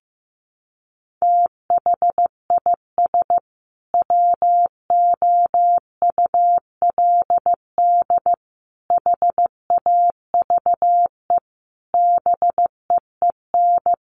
Morse code, heard as a keyed tone, about 15 wpm, 700 Hz. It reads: THIS WOULD HAVE BEEN
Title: Sentences from Top 800 Words 15wpm